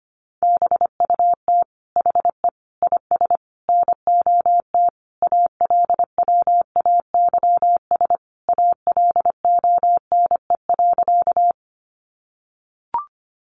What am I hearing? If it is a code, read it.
Morse code, heard as a keyed tone, about 25 wpm, 700 Hz. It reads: BUT HE IS NOT ALWAYS ALONE.
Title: Sentences from Top 1000 Words 25wpm